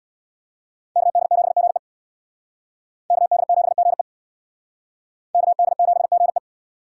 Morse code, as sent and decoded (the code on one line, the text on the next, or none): -.- -.. --... --. / -.- -.. --... --. / -.- -.. --... --. .
KD7GE KD7GE KD7GE